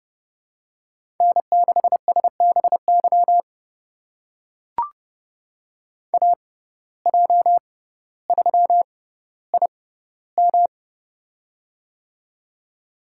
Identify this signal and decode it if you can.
Morse code, heard as a keyed tone, about 30 wpm, spaced as for 15 wpm, 700 Hz. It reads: N6SBY AJ3IM
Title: USA Call Signs - 5 Characters - 6x Character Spacing - 30wpm